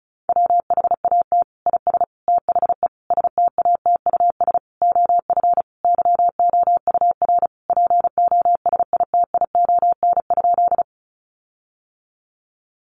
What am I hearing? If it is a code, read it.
Morse code, heard as a keyed tone, about 34 wpm, 700 Hz. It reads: WHAT IS THE STATUS OF YOUR POSITION?
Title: Sentences from Top 1400 Words 35wpm